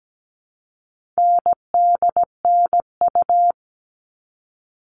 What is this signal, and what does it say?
Morse code, heard as a keyed tone, about 17 wpm, 700 Hz. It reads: NDNU
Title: All Cut Numbers - 4-Digit 17wpm